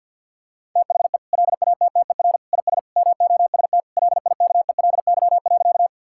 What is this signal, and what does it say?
Morse code, heard as a keyed tone, about 50 wpm, 700 Hz. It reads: THE LATTER IS MOST LIKELY.
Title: Sentences from Top 1600 Words 50wpm